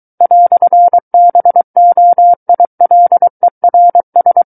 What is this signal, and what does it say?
Morse code, heard as a keyed tone, about 23 wpm, 700 Hz. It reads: "BOILERS
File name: Book One / 14-Chapter 14 - 23 WPM